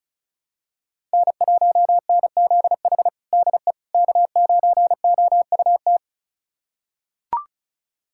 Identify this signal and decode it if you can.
Morse code, heard as a keyed tone, about 34 wpm, 700 Hz. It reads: N1NZH DE K9OUT
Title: USA Callsign DE Callsign - 5 Characters - 35wpm